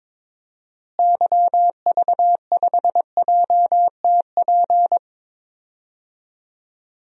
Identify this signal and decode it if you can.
Morse code, heard as a keyed tone, about 22 wpm, 700 Hz. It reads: YV5JTP
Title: DXCC-100 Call Signs - 6 Characters - Repeated 3x - 22wpm